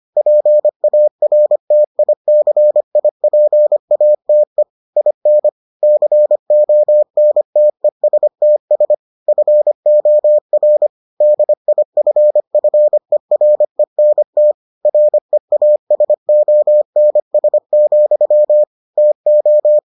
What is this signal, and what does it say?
Morse code, heard as a keyed tone, about 25 wpm, 600 Hz. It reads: PARTICIPATE IN CONTESTS FOR DIFFERENT REASONS, TO